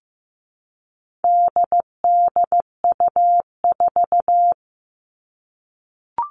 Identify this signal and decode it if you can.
Morse code, heard as a keyed tone, about 15 wpm, 700 Hz. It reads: DDU4
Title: All Cut Numbers - 4-Digit 15wpm